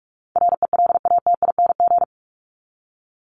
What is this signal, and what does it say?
Morse code, heard as a keyed tone, about 44 wpm, 700 Hz. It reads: RELATING
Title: ING Suffix - Top 100 Words with Base Words 45wpm